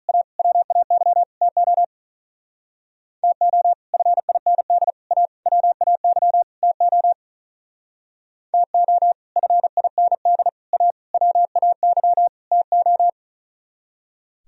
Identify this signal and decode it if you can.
Morse code, heard as a keyed tone, about 45 wpm, 700 Hz. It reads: A WAY TO TO FIND A WAY TO TO FIND A WAY TO